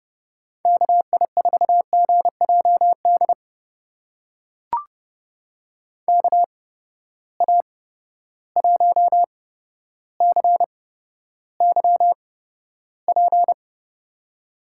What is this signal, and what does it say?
Morse code, heard as a keyed tone, about 30 wpm, spaced as for 13 wpm, 700 Hz. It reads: KI4GJD KA1CYP
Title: USA Call Signs - 6 Characters - 8x Character Spacing - 30wpm